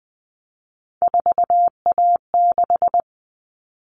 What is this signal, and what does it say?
Morse code, heard as a keyed tone, about 20 wpm, 700 Hz. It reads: 4A6
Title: All Cut Numbers - 3-Digit 20wpm